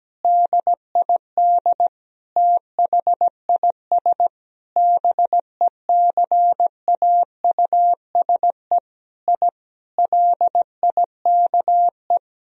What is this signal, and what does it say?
Morse code, heard as a keyed tone, about 17 wpm, 700 Hz. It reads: DID THIS BECAUSE I LIKE
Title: Sentences from Top 1000 Words - Review Part 3 - 17wpm